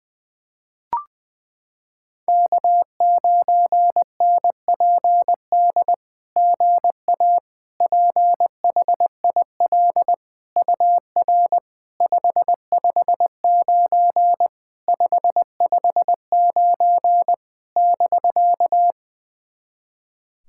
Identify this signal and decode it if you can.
Morse code, heard as a keyed tone, about 20 wpm, 700 Hz. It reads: K9NPD GA PHIL UR 559 559 <BK>
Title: SOTA QSO Practice 20wpm